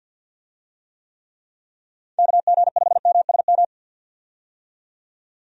Morse code, heard as a keyed tone, about 51 wpm, 700 Hz. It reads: KG5MSM
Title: USA Call Signs - 6 Characters - 50wpm